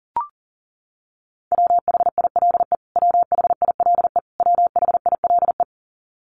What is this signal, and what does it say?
Morse code, heard as a keyed tone, about 40 wpm, 700 Hz. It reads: WHILE WHILE WHILE
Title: Top 500 Words Repeated 40wpm